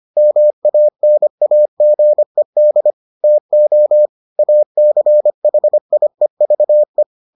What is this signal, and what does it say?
Morse code, heard as a keyed tone, about 25 wpm, 600 Hz. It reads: MANAGED TO ACHIEVE